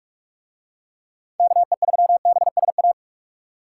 Morse code, heard as a keyed tone, about 44 wpm, 700 Hz. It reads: KE3BSA